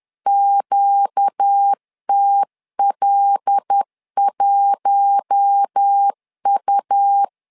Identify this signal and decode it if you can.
Morse code, heard as a keyed tone, about 10 wpm, 800 Hz. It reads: QTL1U
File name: LEZIONI / 5 LBVC / 5 L 10wpm 2259